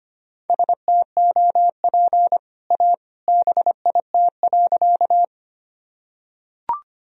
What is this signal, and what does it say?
Morse code, heard as a keyed tone, about 25 wpm, 700 Hz. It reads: STOP A BIT.
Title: Sentences from Top 1000 Words - Review Part 1 - 25wpm